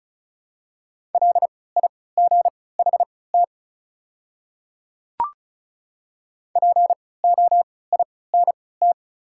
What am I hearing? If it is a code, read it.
Morse code, heard as a keyed tone, about 36 wpm, spaced as for 20 wpm, 700 Hz. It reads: LIGHT POINT
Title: 3x Character Spacing - Top 300 Words - 35wpm